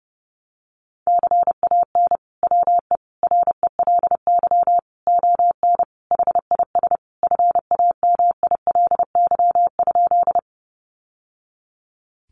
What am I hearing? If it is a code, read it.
Morse code, heard as a keyed tone, about 30 wpm, 700 Hz. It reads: CAN WE RELY ON HIS FAMILY?
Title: Sentences from Top 1200 Words 30wpm